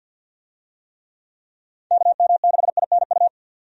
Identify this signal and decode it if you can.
Morse code, heard as a keyed tone, about 53 wpm, 700 Hz. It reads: KM6INU
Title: USA Call Signs - 6 Characters - 50wpm